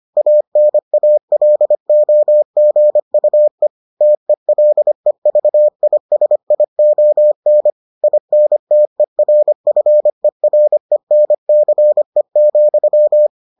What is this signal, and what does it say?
Morse code, heard as a keyed tone, about 25 wpm, 600 Hz. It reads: ANALOGUE TELEVISION INTERFERENCE,